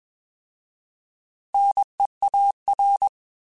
Morse code, heard as a keyed tone, about 21 wpm, 800 Hz. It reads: NEAR
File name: Alphabetical / Alphabetical mp3s / Ns @20